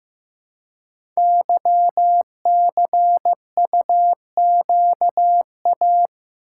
Morse code, heard as a keyed tone, about 15 wpm, 700 Hz. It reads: YCUQA
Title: Code Groups - 5 Letters - 15wpm